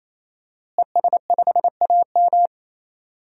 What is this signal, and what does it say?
Morse code, heard as a keyed tone, about 28 wpm, 700 Hz. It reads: ES5AM
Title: DXCC-100 Call Signs - 5 Characters - Repeated 3x - 28wpm